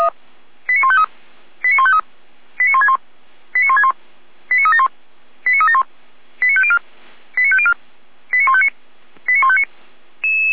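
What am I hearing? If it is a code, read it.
Five-tone selective call (ZVEI), heard as quick runs of five tones, tones of about 70 ms. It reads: 89153 89153 89161 89161 89371 89371 89504 89504 89159 89159
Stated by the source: ZVEI five-tone selective call received by SDR at fire brigade Orlamünde (feuerwehr-orlamuende).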